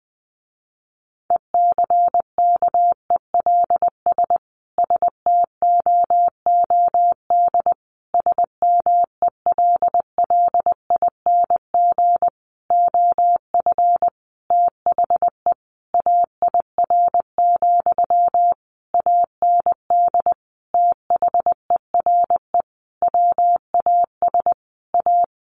Morse code, heard as a keyed tone, about 20 wpm, 700 Hz. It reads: ECKELS STOOD SMELLING OF THE AIR, AND THERE WAS A
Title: A Sound Of Thunder 20wpm